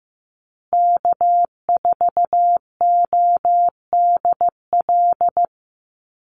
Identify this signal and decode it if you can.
Morse code, heard as a keyed tone, about 15 wpm, 700 Hz. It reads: K4ODL
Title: Call Signs Repeated 3x - 15wpm